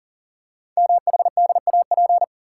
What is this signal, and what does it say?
Morse code, heard as a keyed tone, about 44 wpm, 700 Hz. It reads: MHDAP
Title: Code Groups - 5 Letters - 40wpm